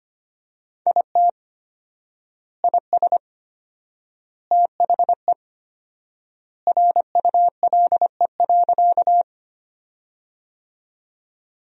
Morse code, heard as a keyed tone, about 25 wpm, 700 Hz. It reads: IT IS THE RULE.
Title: 4x Word-Spacing - Sentences from Top 400 Words - 25wpm